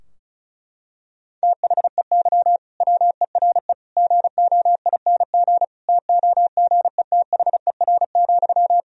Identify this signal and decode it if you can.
Morse code, heard as a keyed tone, about 36 wpm, 700 Hz. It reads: THEY WERE GOING TOGETHER,
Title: Sentences from Top 1000 Words - Review Part 4 - 35wpm